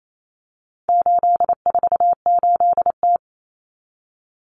Morse code, heard as a keyed tone, about 28 wpm, 700 Hz. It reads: 848T
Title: Common Cut Numbers - 4-Digit 28wpm